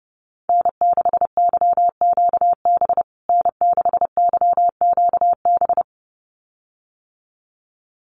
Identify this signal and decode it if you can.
Morse code, heard as a keyed tone, about 30 wpm, 700 Hz. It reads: N6YQB N6YQB
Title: USA Call Signs Repeated 2x 30wpm